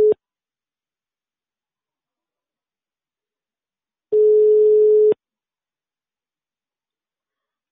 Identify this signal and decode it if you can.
Ringback tone (Europe, 425 Hz).